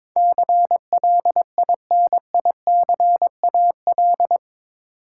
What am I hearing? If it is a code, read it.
Morse code, heard as a keyed tone, about 22 wpm, 700 Hz. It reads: CLINICAL
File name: practice / 22wpm / Eight Letter Words - 22wpm